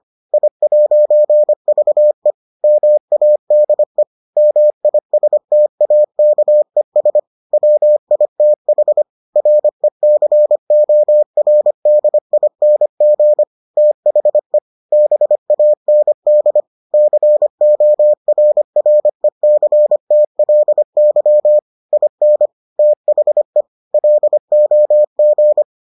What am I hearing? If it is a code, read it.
Morse code, heard as a keyed tone, about 25 wpm, 600 Hz. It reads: I'VE MADE MISTAKES WITH RECORDING THE BAND CORRECTLY IN THE LOG